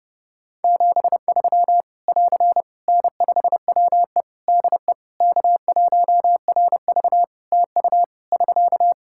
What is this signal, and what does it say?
Morse code, heard as a keyed tone, about 30 wpm, 700 Hz. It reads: 73 <AR> N5WE DE K1RV TU <SK>
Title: Practice QSO - Exchange 4 - Ending - 30wpm